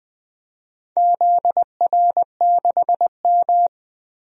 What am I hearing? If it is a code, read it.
Morse code, heard as a keyed tone, about 20 wpm, 700 Hz. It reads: ZR6M
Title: DXCC-100 Call Signs - 4 Characters - Repeated 2x - 20wpm